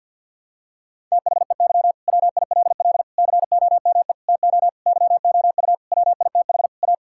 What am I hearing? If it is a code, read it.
Morse code, heard as a keyed tone, about 50 wpm, 700 Hz. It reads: THEY WILL COME TO YOU WITH A
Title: Sentences from Top 1600 Words 50wpm